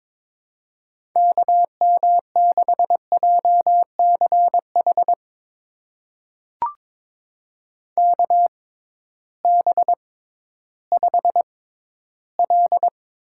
Morse code, heard as a keyed tone, about 22 wpm, spaced as for 13 wpm, 700 Hz. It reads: KM6JCH KB5L
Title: USA Call Signs - 6 Characters - 6x Character Spacing - 22wpm